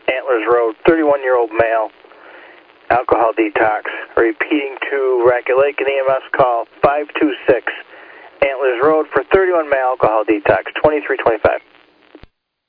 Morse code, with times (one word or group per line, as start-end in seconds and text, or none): none